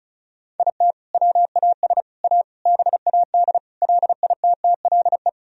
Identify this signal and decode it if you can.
Morse code, heard as a keyed tone, about 36 wpm, 700 Hz. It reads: IT WAS A BAD LITTLE
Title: Sentences from Top 1000 Words - Review Part 1 - 35wpm